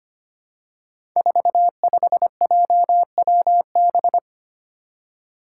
Morse code, heard as a keyed tone, about 25 wpm, 700 Hz. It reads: 45JWB
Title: Five Challenging Letter-Number Combinations 25wpm